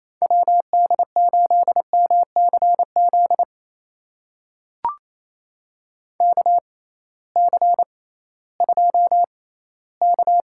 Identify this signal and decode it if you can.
Morse code, heard as a keyed tone, about 28 wpm, spaced as for 16 wpm, 700 Hz. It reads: WD8MCZ KC2K
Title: USA Call Signs - 6 Characters - 6x Character Spacing - 28wpm